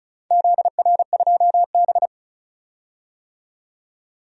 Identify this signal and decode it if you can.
Morse code, heard as a keyed tone, about 36 wpm, 700 Hz. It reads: ZR2B
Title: DXCC-100 Call Signs - 4 Characters - 35wpm